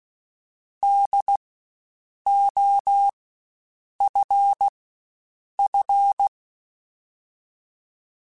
Morse code, heard as a keyed tone, about 16 wpm, spaced as for 7 wpm, 800 Hz. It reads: DOFF
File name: Spaced Alphas / Spaced Alphas mp3s / Ds sp @15